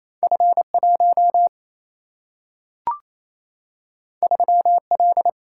Morse code, heard as a keyed tone, about 28 wpm, 700 Hz. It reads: F1 3L
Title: Single-Letter Single-Number 28wpm